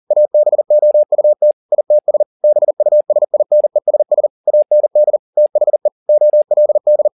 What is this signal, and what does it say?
Morse code, heard as a keyed tone, about 40 wpm, 600 Hz. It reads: ABOUT ITS BUSINESS AND THE OLD
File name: the old man and the sea 2 40wpm